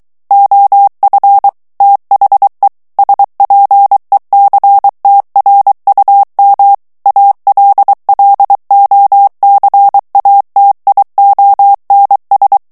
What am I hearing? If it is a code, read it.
Morse code, heard as a keyed tone, about 23 wpm, 800 Hz. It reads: OF THE SPECTRUM ALLOCATIONS
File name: ANGLAIS texte clair 3 vitesse 22 mots minute